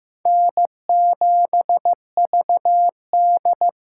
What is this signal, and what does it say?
Morse code, heard as a keyed tone, about 15 wpm, 700 Hz. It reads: N7VD
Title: USA Call Signs - 4 Characters - 8x Character Spacing - 15wpm